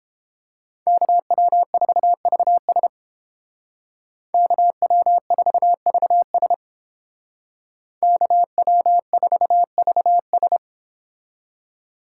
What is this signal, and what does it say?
Morse code, heard as a keyed tone, about 31 wpm, 700 Hz. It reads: KW4VS KW4VS KW4VS